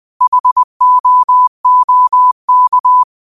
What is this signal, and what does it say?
Morse code, heard as a keyed tone, about 20 wpm, 1000 Hz. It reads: HOOK